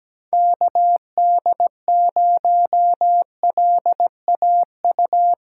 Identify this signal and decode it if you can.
Morse code, heard as a keyed tone, about 17 wpm, 700 Hz. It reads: KD0LAU
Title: USA Callsign DE Callsign 17wpm